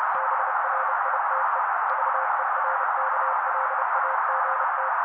Morse code, heard as a keyed tone, about 30 wpm, 530 Hz. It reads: BRITE VERKLAGT
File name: daily / 29wpm noise